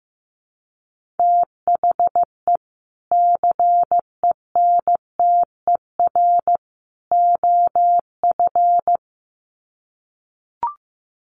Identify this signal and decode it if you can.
Morse code, heard as a keyed tone, about 15 wpm, 700 Hz. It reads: THE CENTER OF
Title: Sets of 3 Words 15wpm